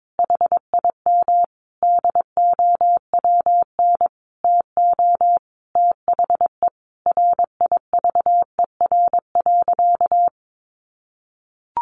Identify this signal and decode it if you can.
Morse code, heard as a keyed tone, about 22 wpm, 700 Hz. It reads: HIM DOWN TO THE RIVER.
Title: Sentences from Top 1500 Words 22wpm